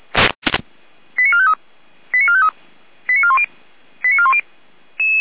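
Five-tone selective call (ZVEI), heard as quick runs of five tones, tones of about 70 ms. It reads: 89453 89453 89410 89410
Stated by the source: ZVEI five-tone selective call, received by SDR.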